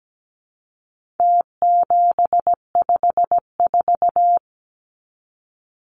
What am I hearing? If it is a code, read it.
Morse code, heard as a keyed tone, about 17 wpm, 700 Hz. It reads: T754
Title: Numbers - Contest Style - 4-Digit 17wpm